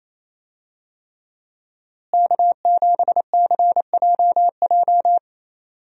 Morse code, heard as a keyed tone, about 28 wpm, 700 Hz. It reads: K7CJJ
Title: USA Call Signs - 5 Characters - Repeated 3x - 28wpm